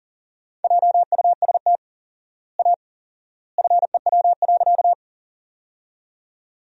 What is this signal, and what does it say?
Morse code, heard as a keyed tone, about 37 wpm, 700 Hz. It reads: JUST A FEW.
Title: 4x Word-Spacing - Sentences from Top 300 Words - 40wpm